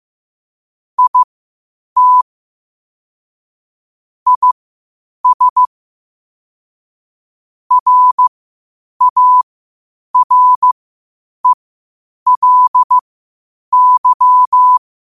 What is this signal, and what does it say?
Morse code, heard as a keyed tone, about 15 wpm, spaced as for 8 wpm, 1000 Hz. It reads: IT IS RARELY